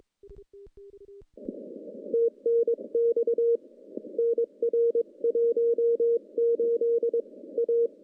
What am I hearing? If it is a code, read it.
Morse code, heard as a keyed tone, about 22 wpm, 470 Hz. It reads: TNX NR 18 A
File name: contests / CW-SS-k5tr- -0100